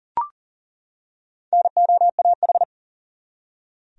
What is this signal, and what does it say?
Morse code, heard as a keyed tone, about 40 wpm, 700 Hz. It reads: NOAH
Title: USA Common Names - Max 4 Letters - 40wpm